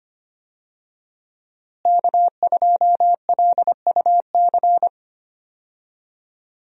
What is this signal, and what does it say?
Morse code, heard as a keyed tone, about 25 wpm, 700 Hz. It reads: K2LUC
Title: USA Call Signs - 5 Characters - Repeated 3x - 25wpm